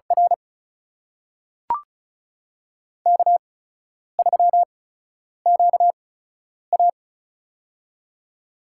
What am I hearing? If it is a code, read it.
Morse code, heard as a keyed tone, about 36 wpm, spaced as for 10 wpm, 700 Hz. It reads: R K3QA
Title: USA Call Signs - 4 Characters - 8x Character Spacing - 35wpm